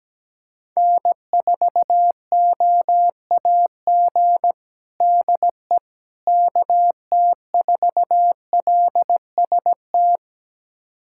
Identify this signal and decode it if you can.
Morse code, heard as a keyed tone, about 17 wpm, 700 Hz. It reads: N4OAG DE KT4LST
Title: USA Callsign DE Callsign 17wpm